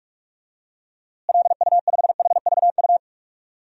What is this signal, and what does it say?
Morse code, heard as a keyed tone, about 46 wpm, 700 Hz. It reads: LU5HVU
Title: DXCC-100 Call Signs - 6 Characters - Repeated 2x - 45wpm